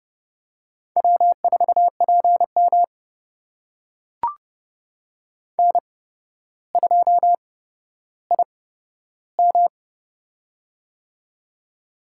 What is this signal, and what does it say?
Morse code, heard as a keyed tone, about 30 wpm, spaced as for 13 wpm, 700 Hz. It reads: W4PM N2IM
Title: USA Call Signs - 4 Characters - 8x Character Spacing - 30wpm